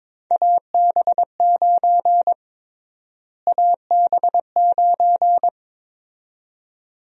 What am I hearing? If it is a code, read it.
Morse code, heard as a keyed tone, about 21 wpm, 700 Hz. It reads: AB9 AB9
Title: USA Call Sign Prefixes - Repeated 2x - 22wpm